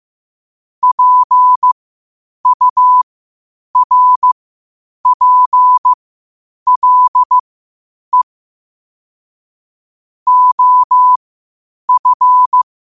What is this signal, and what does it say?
Morse code, heard as a keyed tone, about 15 wpm, spaced as for 8 wpm, 1000 Hz. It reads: PURPLE OF
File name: qotdfiles / 8wpm